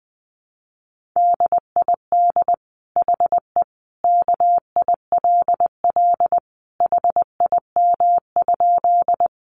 Morse code, heard as a keyed tone, about 20 wpm, 700 Hz. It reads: DID HE KILL HIM?
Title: Sentences from Top 1400 Words 20wpm